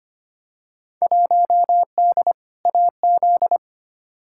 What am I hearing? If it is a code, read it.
Morse code, heard as a keyed tone, about 25 wpm, 700 Hz. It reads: 1D AZ
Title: Field Day Exchanges - 25wpm